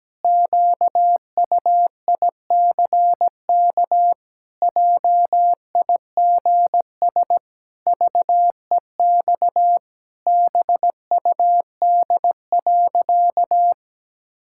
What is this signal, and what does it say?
Morse code, heard as a keyed tone, about 17 wpm, 700 Hz. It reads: QUICK JIGS VEX BUD.
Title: Pangram Challenge v2 - 17wpm